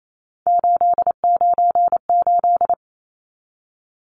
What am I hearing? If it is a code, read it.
Morse code, heard as a keyed tone, about 28 wpm, 700 Hz. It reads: 898
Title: Numbers - 3-Digit 28wpm